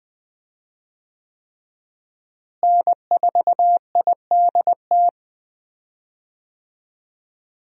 Morse code, heard as a keyed tone, about 20 wpm, 700 Hz. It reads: N4IDT
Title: USA Call Signs - 5 Characters - Repeated 3x - 20wpm